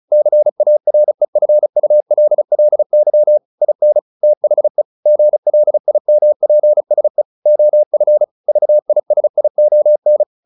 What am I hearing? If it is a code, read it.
Morse code, heard as a keyed tone, about 34 wpm, 600 Hz. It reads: CAREFULLY IN THE GLIMPSE OF VISION
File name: the old man and the sea 2 35wpm